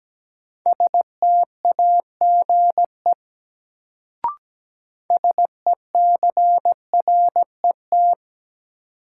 Morse code, heard as a keyed tone, about 17 wpm, 700 Hz. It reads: STAGE SECRET
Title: Sentences from Top 1200 Words - New Word Drill - 17wpm